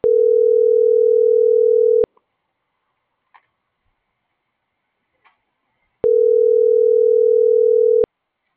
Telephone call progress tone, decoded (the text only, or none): ringback tone